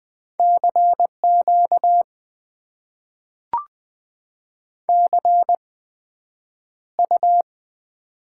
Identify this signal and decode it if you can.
Morse code, heard as a keyed tone, about 20 wpm, spaced as for 9 wpm, 700 Hz. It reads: CQ CU
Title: Top 90 QSO Elements - 8x Character Spacing - 20wpm